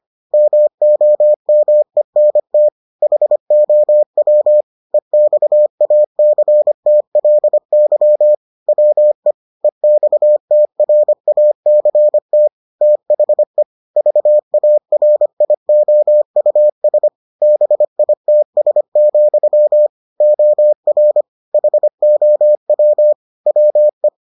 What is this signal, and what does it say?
Morse code, heard as a keyed tone, about 25 wpm, 600 Hz. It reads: MOMENT HOW EXACTLY WE EXTRACT THE VARIOUS BITS, OR HOW WE